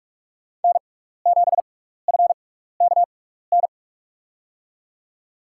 Morse code, heard as a keyed tone, about 46 wpm, spaced as for 15 wpm, 700 Hz. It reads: N7FKN
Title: USA Call Signs - 5 Characters - 6x Character Spacing - 45wpm